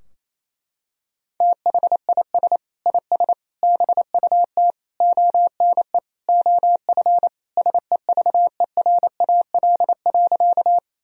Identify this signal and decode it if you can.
Morse code, heard as a keyed tone, about 27 wpm, 700 Hz. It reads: THIS IS BUT ONE OF SEVERAL.
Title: Sentences from Top 1000 Words - Review Part 2 - 28wpm